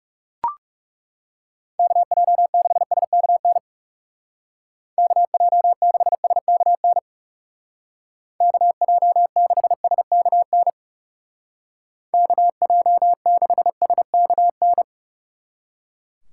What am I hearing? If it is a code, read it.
Morse code, heard as a keyed tone, about 37 wpm, 700 Hz. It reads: KJ6SKN KJ6SKN KJ6SKN KJ6SKN
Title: Speed Racing - USA 2x3 Call Signs 30wpm